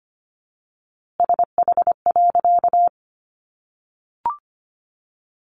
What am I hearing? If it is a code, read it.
Morse code, heard as a keyed tone, about 25 wpm, 700 Hz. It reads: SH.
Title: A Sound Of Thunder 25wpm